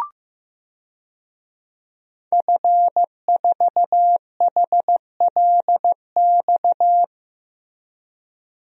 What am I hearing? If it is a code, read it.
Morse code, heard as a keyed tone, about 15 wpm, 700 Hz. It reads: F4HLX
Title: DXCC-100 Call Signs - 5 Characters - Repeated 2x - 15wpm